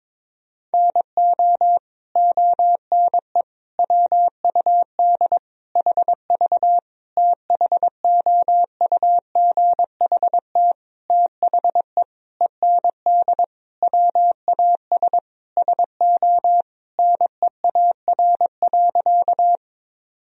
Morse code, heard as a keyed tone, about 22 wpm, 700 Hz. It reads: NO ONE WUD HV THOUGHT THE END WAS SO NEAR.
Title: Sentences from Top 300 Words and Abbreviations 22wpm